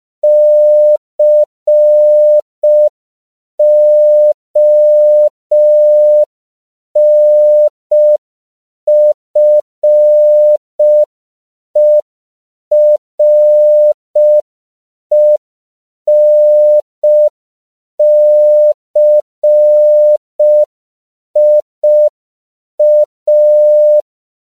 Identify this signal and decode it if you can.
Morse code, heard as a keyed tone, about 5 wpm, 600 Hz. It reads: CONFERENCIA
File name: CW / 5PPM